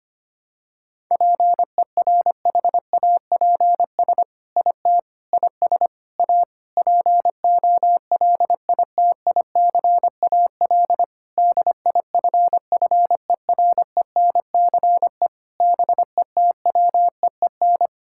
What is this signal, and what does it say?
Morse code, heard as a keyed tone, about 25 wpm, 700 Hz. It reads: PERHAPS IT IS A POLITICAL DIFFERENCE BETWEEN
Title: Sentences from Top 1000 Words - Review Part 5 - 25wpm